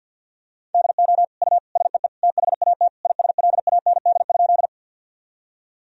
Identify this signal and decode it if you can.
Morse code, heard as a keyed tone, about 50 wpm, 700 Hz. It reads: DO U SEE THAT ISLAND?
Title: Sentences from Top 500 Words and Abbreviations 50wpm